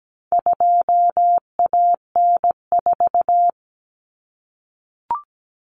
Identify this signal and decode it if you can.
Morse code, heard as a keyed tone, about 17 wpm, 700 Hz. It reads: 2AN4
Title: Common Cut Numbers - 4-Digit 17wpm